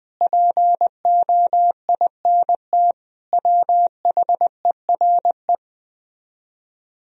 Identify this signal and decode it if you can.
Morse code, heard as a keyed tone, about 20 wpm, 700 Hz. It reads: POINT WHERE
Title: Speed-Racing - Sets of 4 Words 17wpm